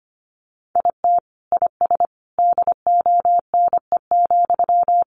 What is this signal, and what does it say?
Morse code, heard as a keyed tone, about 25 wpm, 700 Hz. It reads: IT IS DONE,
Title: Sentences from Top 600 Words 25wpm